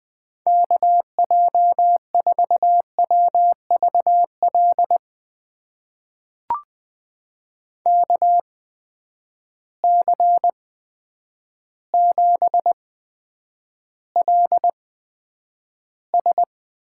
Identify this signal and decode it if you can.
Morse code, heard as a keyed tone, about 20 wpm, spaced as for 9 wpm, 700 Hz. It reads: KJ4WVL KC7LS
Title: USA Call Signs - 6 Characters - 8x Character Spacing - 20wpm